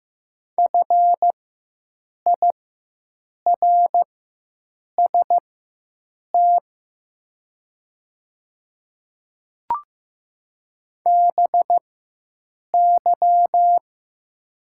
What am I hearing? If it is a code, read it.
Morse code, heard as a keyed tone, about 15 wpm, spaced as for 7 wpm, 690 Hz. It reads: FIRST BY
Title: 4x Character Spacing - Top 500 Words - 15wpm